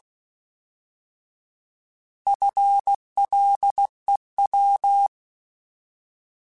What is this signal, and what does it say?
Morse code, heard as a keyed tone, about 16 wpm, 800 Hz. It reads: FLEW
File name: Alphabetical / Alphabetical mp3s / Fs @15